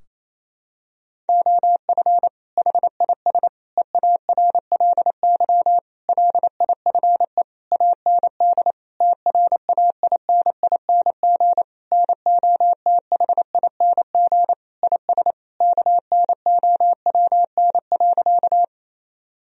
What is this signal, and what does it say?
Morse code, heard as a keyed tone, about 28 wpm, 700 Hz. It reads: OF HIS EARLY LIFE AND TRAINING NOTHING IS KNOWN.
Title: Sentences from Top 1000 Words - Review Part 4 - 28wpm